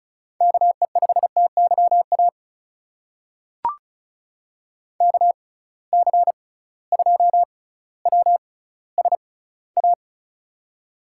Morse code, heard as a keyed tone, about 36 wpm, spaced as for 18 wpm, 700 Hz. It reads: KE5TYA KC2WSA
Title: USA Call Signs - 6x Character Spacing - 35wpm